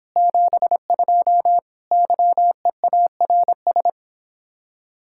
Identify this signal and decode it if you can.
Morse code, heard as a keyed tone, about 25 wpm, 700 Hz. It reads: 72 YEARS